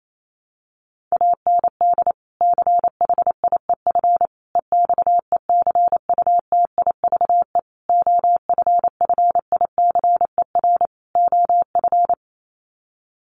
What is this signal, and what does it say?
Morse code, heard as a keyed tone, about 28 wpm, 700 Hz. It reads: AND CHIEF EXECUTIVE OFFICER OF
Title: Sets of 5 Words Encore2 28wpm